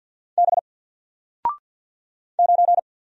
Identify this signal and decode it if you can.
Morse code, heard as a keyed tone, about 48 wpm, 700 Hz. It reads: B 9
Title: Single Letter-Number 50wpm